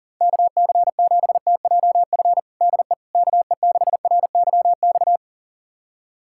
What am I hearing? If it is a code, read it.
Morse code, heard as a keyed tone, about 40 wpm, 700 Hz. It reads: KC7TJF DE KE6RYX
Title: USA Callsign DE Callsign - 6 Characters - 40wpm